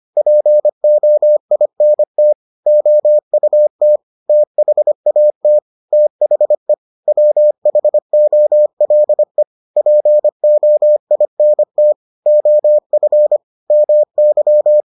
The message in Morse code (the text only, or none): POINT OUT THAT THE WHOLE POINT OF MY